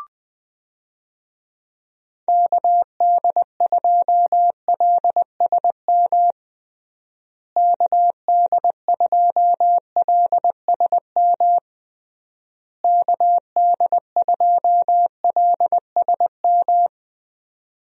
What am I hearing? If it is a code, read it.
Morse code, heard as a keyed tone, about 20 wpm, 700 Hz. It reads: KD2LSM KD2LSM KD2LSM
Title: USA Call Signs - 6 Characters - Repeated 3x - 20wpm